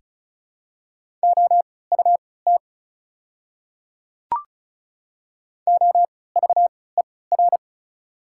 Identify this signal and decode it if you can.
Morse code, heard as a keyed tone, about 34 wpm, spaced as for 20 wpm, 690 Hz. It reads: OUT OVER
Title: 3x Character Spacing - Top 100 Words - 35wpm